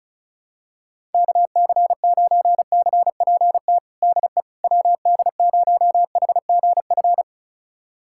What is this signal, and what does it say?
Morse code, heard as a keyed tone, about 34 wpm, 700 Hz. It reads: KC9CPT DE WD0HGF
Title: USA Callsign DE Callsign - 6 Characters - 35wpm